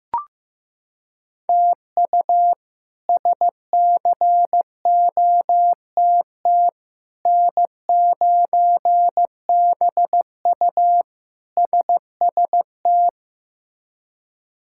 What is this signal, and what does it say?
Morse code, heard as a keyed tone, about 15 wpm, 700 Hz. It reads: TU SCOTT N9BU SST
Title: K1USN SST QSO Practice - 15wpm